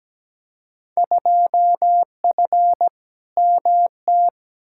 Morse code, heard as a keyed tone, about 17 wpm, 700 Hz. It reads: 2F MT
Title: Field Day Exchanges Encore - 17wpm